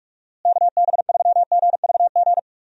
Morse code, heard as a keyed tone, about 47 wpm, 700 Hz. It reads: KB3GVG
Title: USA Call Signs - 6 Characters - Repeated 3x - 45wpm